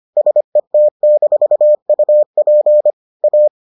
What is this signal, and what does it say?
Morse code, heard as a keyed tone, about 25 wpm, 600 Hz. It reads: SET-UP A